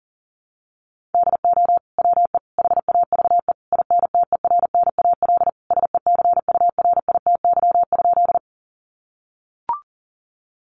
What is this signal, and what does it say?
Morse code, heard as a keyed tone, about 40 wpm, 700 Hz. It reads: DO WE HAVE INTERNAL SECURITY?
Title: Sentences from Top 1400 Words 40wpm